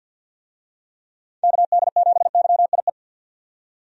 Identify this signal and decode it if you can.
Morse code, heard as a keyed tone, about 48 wpm, 700 Hz. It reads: KD7YIE
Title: USA Call Signs - 6 Characters - 50wpm